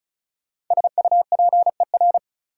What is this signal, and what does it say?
Morse code, heard as a keyed tone, about 45 wpm, 700 Hz. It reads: SUPER